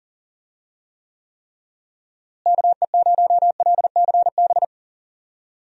Morse code, heard as a keyed tone, about 40 wpm, 700 Hz. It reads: KE0LCB
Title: USA Call Signs - 6 Characters - Repeated 3x - 40wpm